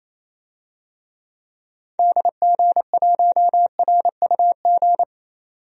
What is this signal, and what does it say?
Morse code, heard as a keyed tone, about 28 wpm, 700 Hz. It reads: DG1RUG
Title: DXCC-100 Call Signs - 6 Characters - 28wpm